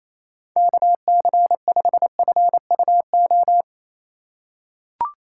Morse code, heard as a keyed tone, about 28 wpm, 700 Hz. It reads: KC5FUO
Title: USA Call Signs - 4x Character Spacing - 28wpm